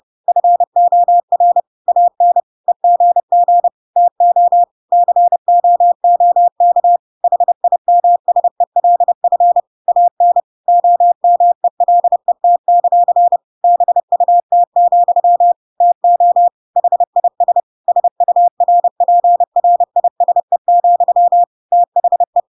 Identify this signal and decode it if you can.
Morse code, heard as a keyed tone, about 30 wpm, 700 Hz. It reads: FOR AN EGG TO COOK HIMSELF AN OMELET; BUT, TO HIS SURPRISE, THE